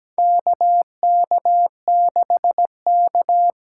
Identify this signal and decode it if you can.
Morse code, heard as a keyed tone, about 17 wpm, 700 Hz. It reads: KK6K
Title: USA Call Signs - 6 Characters - 8x Character Spacing - 17wpm